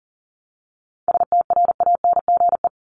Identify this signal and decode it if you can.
Morse code, heard as a keyed tone, about 40 wpm, 700 Hz. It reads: STRANGE
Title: Top 1501-1750 Words 40wpm